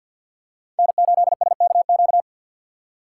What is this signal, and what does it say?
Morse code, heard as a keyed tone, about 47 wpm, 690 Hz. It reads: N8SKQ
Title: USA Call Signs - 3x Character Spacing - 50wpm